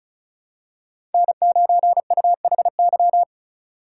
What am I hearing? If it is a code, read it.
Morse code, heard as a keyed tone, about 34 wpm, 700 Hz. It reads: N9UHY
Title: USA Call Signs - 5 Characters - 3x Character Spacing - 35wpm